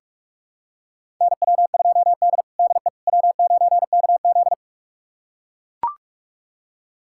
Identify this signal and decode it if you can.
Morse code, heard as a keyed tone, about 44 wpm, 700 Hz. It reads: NW2D DE W9KZ
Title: USA Callsign DE Callsign - 4 Characters - 45wpm